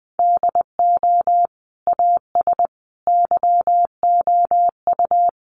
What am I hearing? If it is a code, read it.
Morse code, heard as a keyed tone, about 20 wpm, 700 Hz. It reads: DO AS YOU
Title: Sentences from Top 100 Words 20wpm